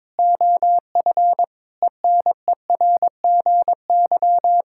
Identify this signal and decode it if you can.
Morse code, heard as a keyed tone, about 22 wpm, 700 Hz. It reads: OF ENERGY
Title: Sentences from Top 1000 Words - Review Part 6 - 22wpm